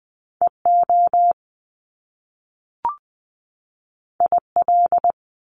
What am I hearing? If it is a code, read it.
Morse code, heard as a keyed tone, about 20 wpm, 700 Hz. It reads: EO IL
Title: Top Two Letter Combinations in Words 20wpm